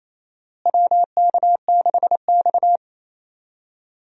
Morse code, heard as a keyed tone, about 28 wpm, 700 Hz. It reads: WK6X
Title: USA Call Signs - 4 Characters - Repeated 2x - 28wpm